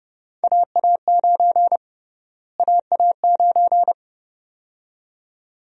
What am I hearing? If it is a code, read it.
Morse code, heard as a keyed tone, about 32 wpm, 700 Hz. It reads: AA9 AA9
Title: USA Call Sign Prefixes - Repeated 2x - 30wpm